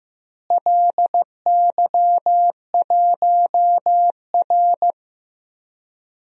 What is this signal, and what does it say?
Morse code, heard as a keyed tone, about 15 wpm, 700 Hz. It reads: LY1R
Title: DXCC-100 Call Signs - 4 Characters - Repeated 2x - 15wpm